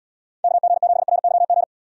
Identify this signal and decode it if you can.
Morse code, heard as a keyed tone, about 74 wpm, 700 Hz. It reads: KG7RYG